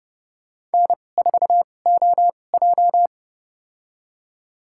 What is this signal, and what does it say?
Morse code, heard as a keyed tone, about 30 wpm, spaced as for 22 wpm, 700 Hz. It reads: N4OJ
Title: USA Call Signs - 4 Characters - 2x Character Spacing - 30wpm